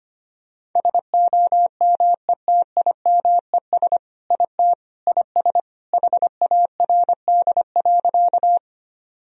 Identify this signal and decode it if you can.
Morse code, heard as a keyed tone, about 25 wpm, 700 Hz. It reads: SOMETIMES IT IS HARD.
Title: Sentences from Top 1000 Words - Review Part 3 - 25wpm